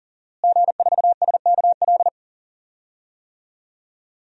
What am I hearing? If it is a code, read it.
Morse code, heard as a keyed tone, about 39 wpm, 700 Hz. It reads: G4SKL
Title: DXCC-100 Call Signs - 5 Characters - Repeated 3x - 40wpm